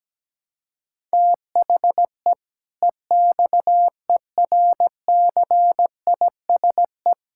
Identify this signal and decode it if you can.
Morse code, heard as a keyed tone, about 17 wpm, 700 Hz. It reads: THE EXERCISE
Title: Sentences from Top 1000 Words - Review Part 6 - 17wpm